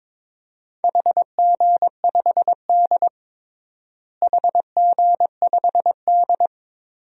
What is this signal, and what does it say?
Morse code, heard as a keyed tone, about 22 wpm, 700 Hz. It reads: HG5D HG5D
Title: DXCC-100 Call Signs - 4 Characters - Repeated 3x - 22wpm